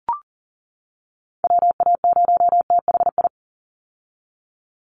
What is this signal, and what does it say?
Morse code, heard as a keyed tone, about 40 wpm, 700 Hz. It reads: WA0THI
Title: USA Call Signs 40wpm